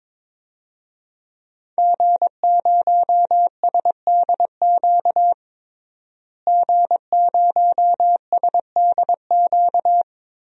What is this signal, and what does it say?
Morse code, heard as a keyed tone, about 22 wpm, 700 Hz. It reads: G0SDQ G0SDQ
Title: DXCC-100 Call Signs - 5 Characters - Repeated 3x - 22wpm